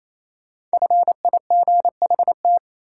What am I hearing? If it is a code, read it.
Morse code, heard as a keyed tone, about 28 wpm, 700 Hz. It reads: FIGHT
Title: Five Letter Words - 28wpm